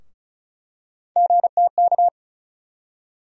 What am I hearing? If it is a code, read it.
Morse code, heard as a keyed tone, about 34 wpm, 700 Hz. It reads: GTK
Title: USA Call Sign Postfixes - 3 Letters 35wpm